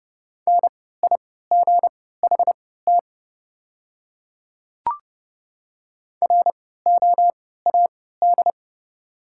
Morse code, heard as a keyed tone, about 30 wpm, spaced as for 17 wpm, 690 Hz. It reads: NIGHT ROAD
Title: 3x Character Spacing - Top 500 Words - 30wpm